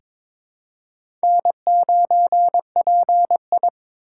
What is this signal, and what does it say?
Morse code, heard as a keyed tone, about 22 wpm, 700 Hz. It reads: N9PI
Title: USA Call Signs - 4 Characters - 4x Character Spacing - 22wpm